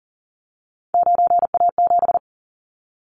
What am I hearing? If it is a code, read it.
Morse code, heard as a keyed tone, about 42 wpm, 700 Hz. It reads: 9A7
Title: Callsign Prefixes for Top 100 DXCC - 40wpm